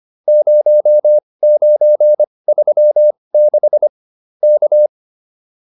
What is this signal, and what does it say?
Morse code, heard as a keyed tone, about 25 wpm, spaced as for 20 wpm, 600 Hz. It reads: 0936 K